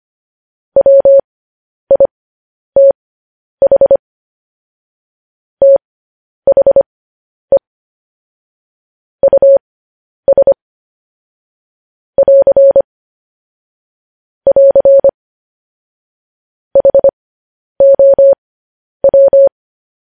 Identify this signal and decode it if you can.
Morse code, heard as a keyed tone, about 25 wpm, spaced as for 10 wpm, 550 Hz. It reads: WITH THE US <AR> <AR> HOW
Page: news headlines in morse code at 10 wpm